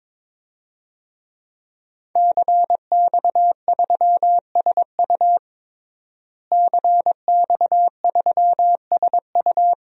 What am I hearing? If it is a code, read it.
Morse code, heard as a keyed tone, about 22 wpm, 700 Hz. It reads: CX3SU CX3SU
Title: DXCC-100 Call Signs - 5 Characters - Repeated 3x - 22wpm